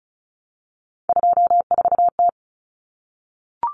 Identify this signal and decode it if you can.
Morse code, heard as a keyed tone, about 34 wpm, 700 Hz. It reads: 24T
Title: Common Cut Numbers - 3-Digit 35wpm